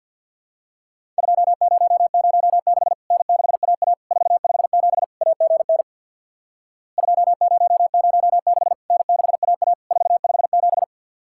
Morse code, heard as a keyed tone, about 52 wpm, 700 Hz. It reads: 2006 N6AA 457 AGN 2006 N6AA 457
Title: Worked All Europe DX Contest - QTC Exchange with Repeats - 50wpm